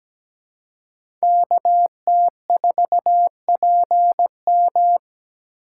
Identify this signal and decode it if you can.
Morse code, heard as a keyed tone, about 17 wpm, 700 Hz. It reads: KT4PM
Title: USA Call Signs - 5 Characters - 2x Character Spacing - 17wpm